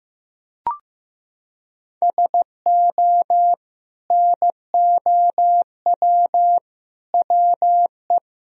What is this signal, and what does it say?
Morse code, heard as a keyed tone, about 15 wpm, 700 Hz. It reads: SO NOW WE
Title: Sentences from Top 1000 Words - Review Part 1 - 15wpm